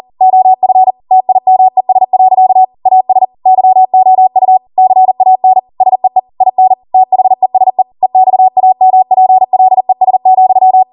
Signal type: Morse code